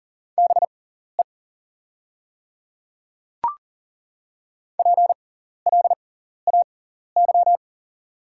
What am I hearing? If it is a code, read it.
Morse code, heard as a keyed tone, about 40 wpm, spaced as for 14 wpm, 700 Hz. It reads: BE PLAY
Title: 6x Character Spacing - Top 300 Words - 40wpm